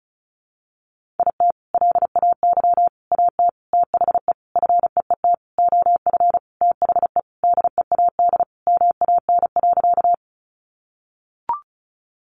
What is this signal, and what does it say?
Morse code, heard as a keyed tone, about 34 wpm, 700 Hz. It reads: IT LAY AT THE FEET OF THE DEAD MAN.
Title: Sentences from Top 1500 Words 35wpm